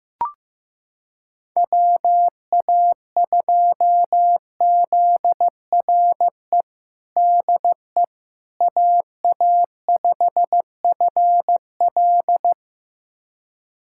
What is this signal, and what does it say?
Morse code, heard as a keyed tone, about 15 wpm, 700 Hz. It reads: WA2ZRE DE AA5FL
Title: USA Callsign DE Callsign 15wpm